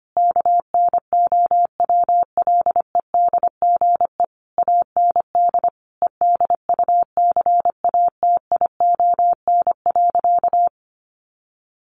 Morse code, heard as a keyed tone, about 25 wpm, 700 Hz. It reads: KNOWLEDGE AND EDUCATION.
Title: Sentences from Top 1500 Words 25wpm